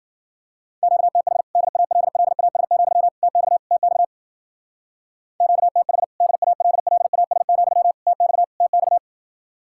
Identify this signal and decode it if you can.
Morse code, heard as a keyed tone, about 50 wpm, 700 Hz. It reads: QTH DALLAS, TX TX QTH DALLAS, TX TX